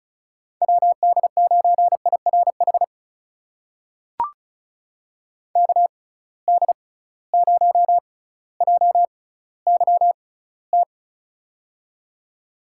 Morse code, heard as a keyed tone, about 34 wpm, spaced as for 18 wpm, 700 Hz. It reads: WD9IRH KD0JYT
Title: USA Call Signs - 6 Characters - 6x Character Spacing - 35wpm